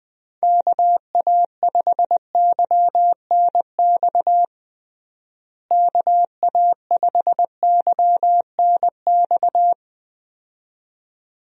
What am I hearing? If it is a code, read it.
Morse code, heard as a keyed tone, about 20 wpm, 700 Hz. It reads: KA5YNX KA5YNX
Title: USA Call Signs - 6 Characters - Repeated 2x - 20wpm